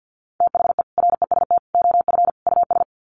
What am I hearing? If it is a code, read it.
Morse code, heard as a keyed tone, about 50 wpm, 700 Hz. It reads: THE REST OF US
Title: Sets of 4 Words 50wpm